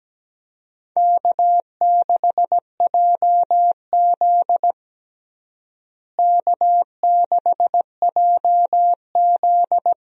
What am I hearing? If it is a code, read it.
Morse code, heard as a keyed tone, about 17 wpm, 700 Hz. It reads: K6JZ K6JZ
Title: USA Call Signs - 4 Characters - Repeated 2x - 17wpm